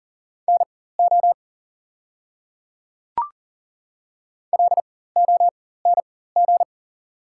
Morse code, heard as a keyed tone, about 40 wpm, spaced as for 19 wpm, 690 Hz. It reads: NO LONG
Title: 4x Character Spacing - Top 100 Words - 40wpm